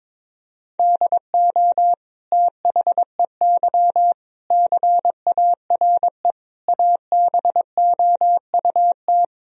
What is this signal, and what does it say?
Morse code, heard as a keyed tone, about 22 wpm, 700 Hz. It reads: DO THEY CARE ABOUT
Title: Sentences from Top 1000 Words - Review Part 3 - 22wpm